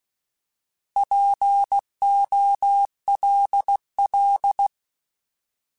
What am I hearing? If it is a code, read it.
Morse code, heard as a keyed tone, about 16 wpm, 800 Hz. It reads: POLL
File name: Alphabetical / Alphabetical mp3s / Ps @15